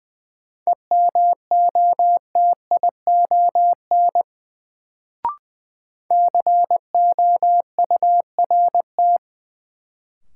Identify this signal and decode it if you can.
Morse code, heard as a keyed tone, about 20 wpm, 700 Hz. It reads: EMOTION COURT
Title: Sentences from Top 1600 Words - New Word Drill - 20wpm